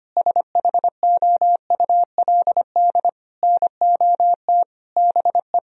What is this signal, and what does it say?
Morse code, heard as a keyed tone, about 25 wpm, 700 Hz. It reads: SHOULD NOT BE
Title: Sentences from Top 1000 Words - Review Part 4 - 25wpm